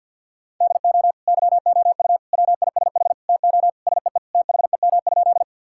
Morse code, heard as a keyed tone, about 50 wpm, 700 Hz. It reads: DO YOU WISH TO SEE THEM?
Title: Sentences from Top 1600 Words 50wpm